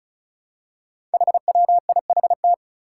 Morse code, heard as a keyed tone, about 36 wpm, 700 Hz. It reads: HWIHT
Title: Five Challenging Letter-Number Combinations 35wpm